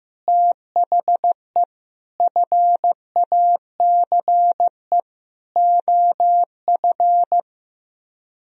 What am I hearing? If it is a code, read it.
Morse code, heard as a keyed tone, about 15 wpm, 700 Hz. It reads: THE FACE OF
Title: Speed-Racing - Sets of 4 Words 15wpm